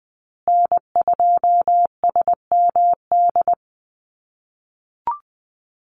Morse code, heard as a keyed tone, about 20 wpm, 700 Hz. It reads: N2SMD
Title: USA Call Signs Repeated 2x 20wpm